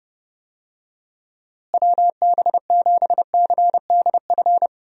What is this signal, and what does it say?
Morse code, heard as a keyed tone, about 30 wpm, 700 Hz. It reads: WB7CDF
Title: USA Call Signs - 6 Characters - Repeated 3x - 30wpm